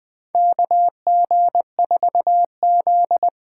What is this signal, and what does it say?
Morse code, heard as a keyed tone, about 20 wpm, 700 Hz. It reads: KG4Z
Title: USA Call Signs - 6 Characters - 8x Character Spacing - 20wpm